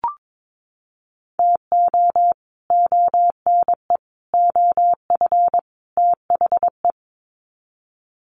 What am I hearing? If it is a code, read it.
Morse code, heard as a keyed tone, about 22 wpm, 700 Hz. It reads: TO ONE OF THE
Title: Sets of 4 Words 22wpm